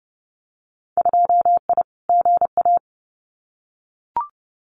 Morse code, heard as a keyed tone, about 31 wpm, 700 Hz. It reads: 2I GA
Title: Winter Field Day Exchanges 30wpm